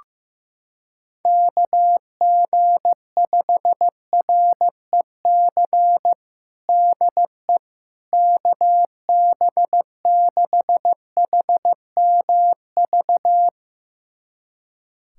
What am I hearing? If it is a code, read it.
Morse code, heard as a keyed tone, about 15 wpm, 700 Hz. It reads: KG5REC DE KB6HMV
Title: USA Callsign DE Callsign - 6 Characters - 15wpm